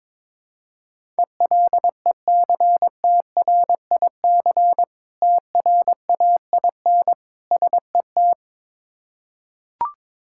Morse code, heard as a keyed tone, about 22 wpm, 700 Hz. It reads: ELECTRIC TRAIN SET
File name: Difficult Words with Hints - 22wpm